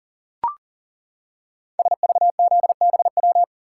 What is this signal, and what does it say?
Morse code, heard as a keyed tone, about 38 wpm, 700 Hz. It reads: SVZBW
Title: Code Groups - 5 Letters - 40wpm